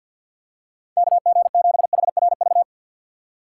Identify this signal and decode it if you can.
Morse code, heard as a keyed tone, about 49 wpm, 700 Hz. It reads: KG7HRV
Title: USA Call Signs - 3x Character Spacing - 50wpm